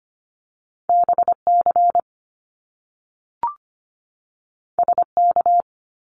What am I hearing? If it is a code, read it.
Morse code, heard as a keyed tone, about 25 wpm, 700 Hz. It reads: BC SK
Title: Canadian Provinces and Territories 25wpm